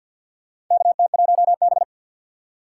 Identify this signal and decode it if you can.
Morse code, heard as a keyed tone, about 52 wpm, 700 Hz. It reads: KT1B
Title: K1USN SST QSO Practice - 50wpm